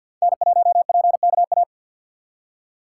Morse code, heard as a keyed tone, about 62 wpm, 700 Hz. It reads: N1PKA
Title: Speed Racing - USA Call Signs 50wpm